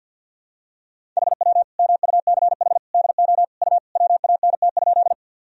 Morse code, heard as a keyed tone, about 46 wpm, 700 Hz. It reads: HW MUCH DO U WANT?
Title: Sentences from Top 200 Words and Abbreviations 50wpm